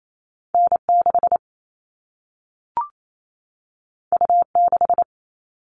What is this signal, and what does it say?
Morse code, heard as a keyed tone, about 28 wpm, 700 Hz. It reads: N6 U6
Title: All Cut Numbers - 2-Digit 28wpm